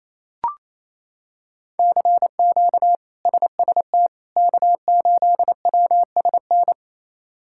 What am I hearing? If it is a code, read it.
Morse code, heard as a keyed tone, about 28 wpm, 700 Hz. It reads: CQ SST K8WSN
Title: K1USN SST QSO Practice - 28wpm